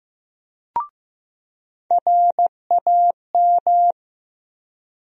Morse code, heard as a keyed tone, about 15 wpm, 700 Hz. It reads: RAM
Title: USA Call Sign Postfixes - 3 Letters 15wpm